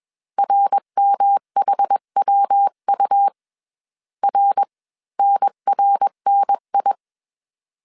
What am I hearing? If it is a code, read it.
Morse code, heard as a keyed tone, about 21 wpm, spaced as for 18 wpm, 800 Hz. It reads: RMHWU RNRNI
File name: CW / DET / 1 3 1500C S B 20wpm 1754